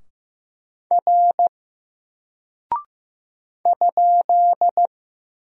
Morse code, heard as a keyed tone, about 15 wpm, 700 Hz. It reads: R ?
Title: Single Character (Letter:Number:Punctuation:Prosign) 15wpm